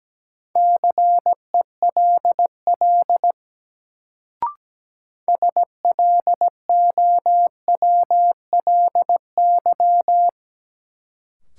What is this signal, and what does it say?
Morse code, heard as a keyed tone, about 17 wpm, 700 Hz. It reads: CELL SLOWLY
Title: Sentences from Top 1100 Words - New Word Drill Alt - 17wpm